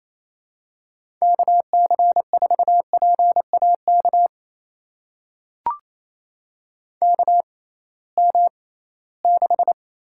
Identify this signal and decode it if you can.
Morse code, heard as a keyed tone, about 28 wpm, spaced as for 18 wpm, 700 Hz. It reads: KC4PAK KM6
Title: USA Call Signs - 6 Characters - 6x Character Spacing - 28wpm